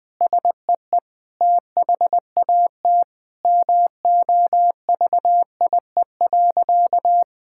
Morse code, heard as a keyed tone, about 20 wpm, 700 Hz. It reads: SEE THAT MOVIE.
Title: Sentences with 5-Word Phrases - 20wpm